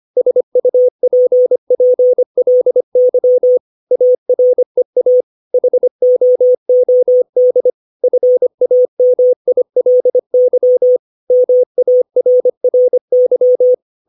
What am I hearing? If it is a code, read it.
Morse code, heard as a keyed tone, about 25 wpm, 500 Hz. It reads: SUPPLY AREA HOOD FAMILY MARRY
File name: 632 words / 632 Words 25WPM